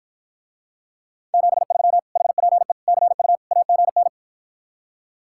Morse code, heard as a keyed tone, about 54 wpm, 700 Hz. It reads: 73 HPE CU AGN